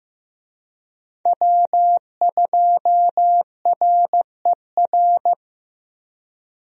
Morse code, heard as a keyed tone, about 15 wpm, 700 Hz. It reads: W2RER
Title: USA Call Signs - 5 Characters - Repeated 2x - 15wpm